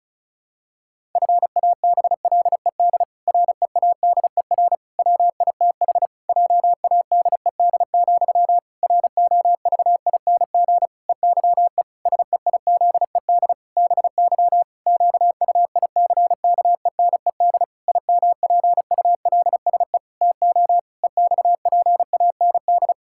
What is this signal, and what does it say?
Morse code, heard as a keyed tone, about 34 wpm, 700 Hz. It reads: FABLED READER WITH JADED, ROVING EYE SEIZED BY QUICKENED IMPULSE TO EXPAND
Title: Pangram Challenge v2 - 35wpm